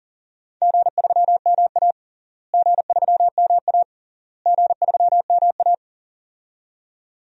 Morse code, heard as a keyed tone, about 40 wpm, 700 Hz. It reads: G3MA G3MA G3MA
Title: DXCC-100 Call Signs - 4 Characters - Repeated 3x - 40wpm